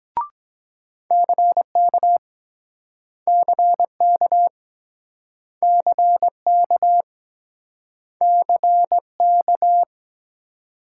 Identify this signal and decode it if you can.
Morse code, heard as a keyed tone, about 23 wpm, 700 Hz. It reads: CK CK CK CK